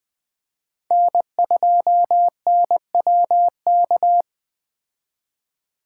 Morse code, heard as a keyed tone, about 20 wpm, 700 Hz. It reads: N2NWK
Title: 2023 POTA Activator Call Signs - 20wpm